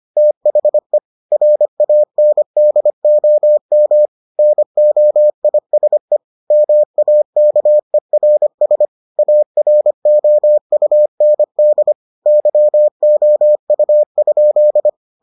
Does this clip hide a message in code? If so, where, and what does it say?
Yes: Morse code from 0.2 to 14.9 s, heard as a keyed tone, about 25 wpm, 600 Hz. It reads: THE RANDOM NOISE MAKERS AROUND YOU?